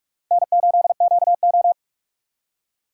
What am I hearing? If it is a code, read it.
Morse code, heard as a keyed tone, about 48 wpm, 700 Hz. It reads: N8QO
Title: USA Call Signs - 4 Characters - 8x Character Spacing - 45wpm